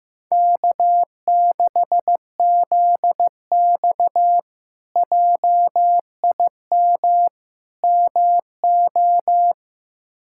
Morse code, heard as a keyed tone, about 15 wpm, 700 Hz. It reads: K6ZX JIM MO
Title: K1USN SST QSO Practice - 15wpm